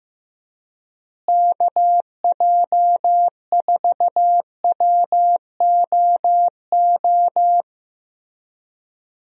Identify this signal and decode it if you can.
Morse code, heard as a keyed tone, about 16 wpm, 700 Hz. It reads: KJ4WOO
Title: USA Call Signs - 6 Characters - Repeated 3x - 15wpm